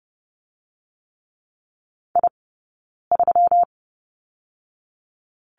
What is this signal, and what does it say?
Morse code, heard as a keyed tone, about 30 wpm, 700 Hz. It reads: I 3
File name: Lesson 20d - Number 3 - Two Characters 30wpm